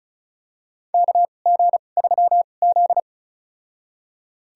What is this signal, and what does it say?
Morse code, heard as a keyed tone, about 36 wpm, spaced as for 26 wpm, 700 Hz. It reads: KG3Z
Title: USA Call Signs - 4 Characters - 2x Character Spacing - 35wpm